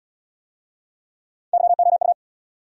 Morse code, heard as a keyed tone, about 68 wpm, 690 Hz. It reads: YOU